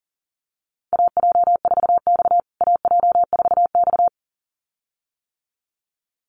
Morse code, heard as a keyed tone, about 37 wpm, 700 Hz. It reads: AJ4X AJ4X
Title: USA Call Signs Repeated 2x 40wpm